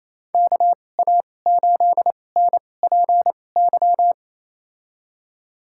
Morse code, heard as a keyed tone, about 27 wpm, spaced as for 20 wpm, 700 Hz. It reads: KA8NPY
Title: USA Call Signs - 6 Characters - 2x Character Spacing - 28wpm